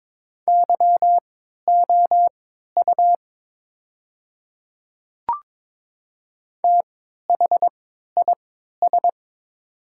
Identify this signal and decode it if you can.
Morse code, heard as a keyed tone, about 21 wpm, spaced as for 12 wpm, 700 Hz. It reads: YOU THIS
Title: 3x Character Spacing - Top 100 Words - 22wpm